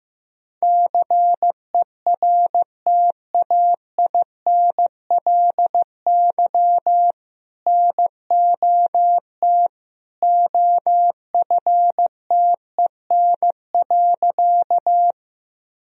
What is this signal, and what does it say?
Morse code, heard as a keyed tone, about 15 wpm, 700 Hz. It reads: CERTAINLY NOT OFTEN.
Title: Sentences from Top 1000 Words - Review Part 1 - 15wpm